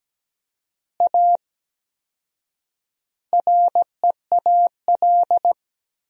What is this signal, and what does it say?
Morse code, heard as a keyed tone, about 17 wpm, 700 Hz. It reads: A REAL
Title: 4x Word-Spacing - Sentences from Top 500 Words - 17wpm